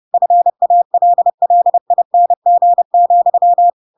Morse code, heard as a keyed tone, about 30 wpm, 700 Hz. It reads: FALLING,
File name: Pinocchio / Pinocchio ch7 30 30 ik1hge